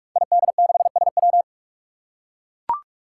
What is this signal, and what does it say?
Morse code, heard as a keyed tone, about 44 wpm, 700 Hz. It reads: ID6SW
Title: Five Challenging Letter-Number Combinations 45wpm